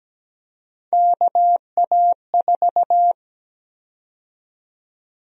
Morse code, heard as a keyed tone, about 17 wpm, 700 Hz. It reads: KA4
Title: USA Call Sign Prefixes - 17wpm